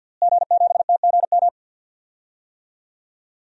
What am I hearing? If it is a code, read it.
Morse code, heard as a keyed tone, about 48 wpm, 700 Hz. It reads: G7TGM
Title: DXCC-100 Call Signs - 5 Characters - Repeated 2x - 50wpm